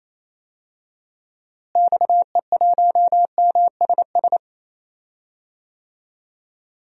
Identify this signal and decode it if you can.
Morse code, heard as a keyed tone, about 28 wpm, 700 Hz. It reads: XE1MSS
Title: DXCC-100 Call Signs - 6 Characters - Repeated 3x - 28wpm